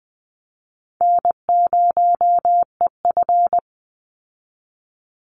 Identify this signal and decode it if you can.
Morse code, heard as a keyed tone, about 20 wpm, 700 Hz. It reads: N0EF
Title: Easy Call Signs Repeated 3x - 20wpm